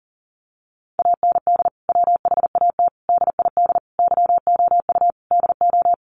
Morse code, heard as a keyed tone, about 40 wpm, 700 Hz. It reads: AND WHAT DID YOU DO
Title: Sentences from Top 300 Words 40wpm